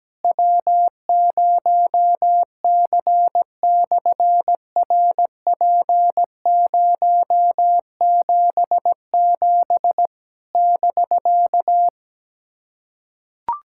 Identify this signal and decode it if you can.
Morse code, heard as a keyed tone, about 17 wpm, 700 Hz. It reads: W0C/RP077 <BK>
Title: SOTA QSO Practice 17wpm